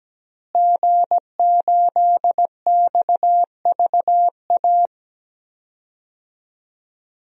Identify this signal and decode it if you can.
Morse code, heard as a keyed tone, about 17 wpm, 700 Hz. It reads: G8XVA
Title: DXCC-100 Call Signs - 5 Characters - Repeated 2x - 17wpm